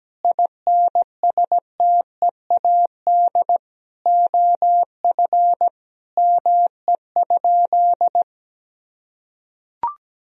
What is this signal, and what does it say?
Morse code, heard as a keyed tone, about 17 wpm, 700 Hz. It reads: INSTEAD OF ME?
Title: Sentences from Top 1000 Words - Review Part 4 - 17wpm